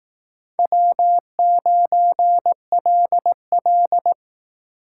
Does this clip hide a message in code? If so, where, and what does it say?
Yes: Morse code from 0.6 to 4.1 s, heard as a keyed tone, about 20 wpm, 700 Hz. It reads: W9LL